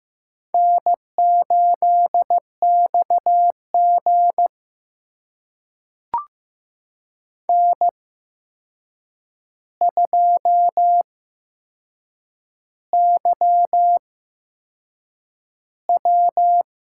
Morse code, heard as a keyed tone, about 15 wpm, spaced as for 6 wpm, 700 Hz. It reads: N8XG N2YW
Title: USA Call Signs - 4 Characters - 8x Character Spacing - 15wpm